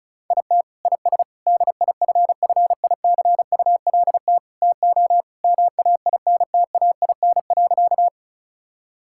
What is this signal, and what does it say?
Morse code, heard as a keyed tone, about 36 wpm, 700 Hz. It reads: IT IS DIFFICULT TO MAINTAIN.
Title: Sentences from Top 1000 Words - Review Part 5 - 35wpm